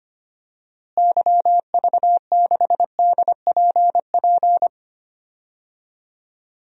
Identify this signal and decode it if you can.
Morse code, heard as a keyed tone, about 25 wpm, 700 Hz. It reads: YV6DPP
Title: DXCC-100 Call Signs - 6 Characters - Repeated 2x - 25wpm